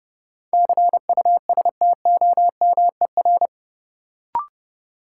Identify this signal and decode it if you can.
Morse code, heard as a keyed tone, about 32 wpm, 700 Hz. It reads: CUSTOMER
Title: Sentences from Top 1100 Words - New Word Drill - 30wpm